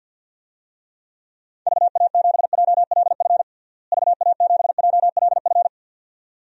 Morse code, heard as a keyed tone, about 52 wpm, 700 Hz. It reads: VA7JLF VA7JLF
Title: DXCC-100 Call Signs - 6 Characters - Repeated 2x - 50wpm